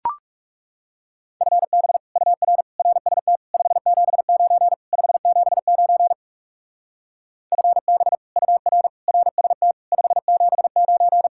Morse code, heard as a keyed tone, about 39 wpm, 700 Hz. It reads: FB UR RST 579 579 FB UR RST 579